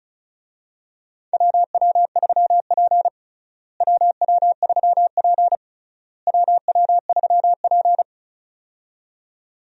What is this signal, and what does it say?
Morse code, heard as a keyed tone, about 34 wpm, 700 Hz. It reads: WW3P WW3P WW3P
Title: USA Call Signs - 4 Characters - Repeated 3x - 35wpm